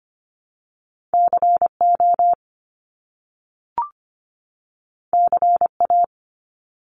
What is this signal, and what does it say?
Morse code, heard as a keyed tone, about 25 wpm, 700 Hz. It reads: CO CA
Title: US States 25wpm